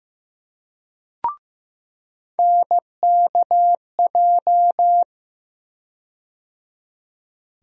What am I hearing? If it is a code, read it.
Morse code, heard as a keyed tone, about 15 wpm, 700 Hz. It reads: NKJ
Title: USA Call Sign Postfixes - 3 Letters 15wpm